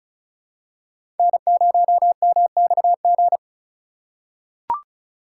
Morse code, heard as a keyed tone, about 36 wpm, 700 Hz. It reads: N0MXG
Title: USA Call Signs - 4x Character Spacing - 35wpm